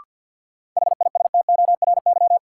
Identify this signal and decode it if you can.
Morse code, heard as a keyed tone, about 52 wpm, 700 Hz. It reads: HISTORY
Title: Sets of 3 Words - New Word Drill - 50wpm